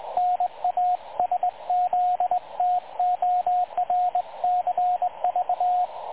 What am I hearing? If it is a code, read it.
Morse code, heard as a keyed tone, about 19 wpm, 720 Hz, with radio noise. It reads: NASZ TORCV
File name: dlazaawansowanych / nagraniazpasma / websdr recording 3565.0kHz